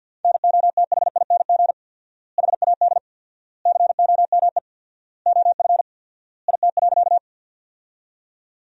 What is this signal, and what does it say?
Morse code, heard as a keyed tone, about 47 wpm, 700 Hz. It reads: NOTHING HAD COME OF IT.
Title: 4x Word-Spacing - Sentences from Top 500 Words - 50wpm